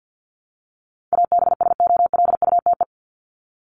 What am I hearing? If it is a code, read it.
Morse code, heard as a keyed tone, about 50 wpm, 700 Hz. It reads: ABSOLUTE
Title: LY Suffix - Top 100 Words with Base Words 50wpm